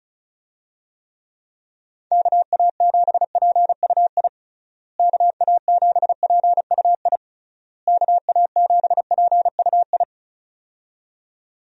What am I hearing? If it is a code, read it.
Morse code, heard as a keyed tone, about 36 wpm, 700 Hz. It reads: KA7PUI KA7PUI KA7PUI
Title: USA Call Signs - 6 Characters - Repeated 3x - 35wpm